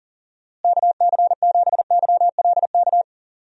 Morse code, heard as a keyed tone, about 40 wpm, 700 Hz. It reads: KC7YLK
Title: USA Call Signs - 6 Characters - Repeated 2x - 40wpm